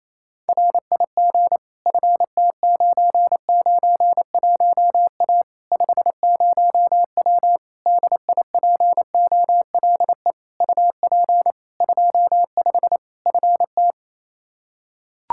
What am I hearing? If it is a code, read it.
Morse code, heard as a keyed tone, about 28 wpm, 700 Hz. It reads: RIG FT991A 50W DIPOLE UP 25 FT
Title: Practice QSO - Exchange 2 - Info - 28wpm